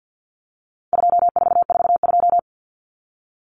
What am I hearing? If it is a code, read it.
Morse code, heard as a keyed tone, about 50 wpm, 700 Hz. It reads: 2442
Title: Numbers - 4-Digit 50wpm